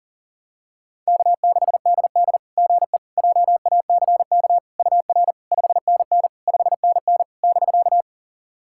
Morse code, heard as a keyed tone, about 43 wpm, 700 Hz. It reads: K6DD GE JACK UR 5NN 5NN <BK>
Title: SOTA QSO Practice 40wpm